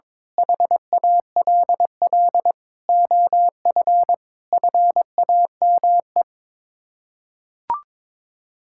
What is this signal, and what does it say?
Morse code, heard as a keyed tone, about 22 wpm, 700 Hz. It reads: HALL OF FAME
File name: practice / 22wpm / Sets of 7 Words - 22wpm